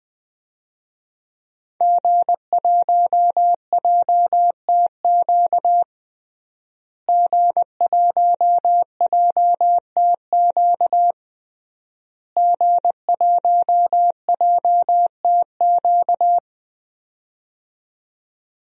Morse code, heard as a keyed tone, about 20 wpm, 700 Hz. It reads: G1JTQ G1JTQ G1JTQ
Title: DXCC-100 Call Signs - 5 Characters - Repeated 3x - 20wpm